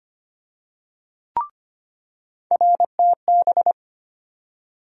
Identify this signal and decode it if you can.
Morse code, heard as a keyed tone, about 25 wpm, 700 Hz. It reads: RTB
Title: USA Call Sign Postfixes - 3 Letters 25wpm